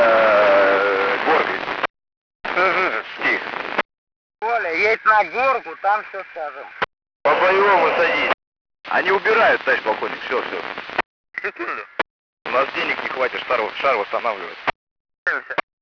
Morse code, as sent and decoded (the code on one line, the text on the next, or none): none